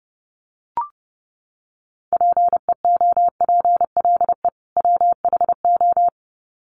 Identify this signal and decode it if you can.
Morse code, heard as a keyed tone, about 30 wpm, 700 Hz. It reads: PEOPLE WHO
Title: Sets of 2 Words 30wpm